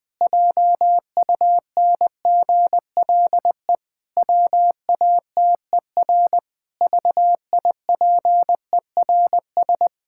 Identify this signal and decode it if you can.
Morse code, heard as a keyed tone, about 20 wpm, 700 Hz. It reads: JUNGLE WATER VIPERS
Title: Pangram Challenge v2 - 20wpm